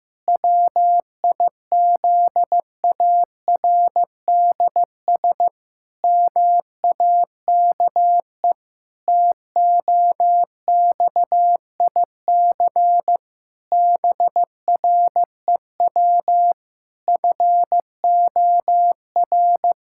Morse code, heard as a keyed tone, about 16 wpm, 700 Hz. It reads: WIZARDS MAKE TOXIC BREW FOR
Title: Pangram Challenge v2 - 15wpm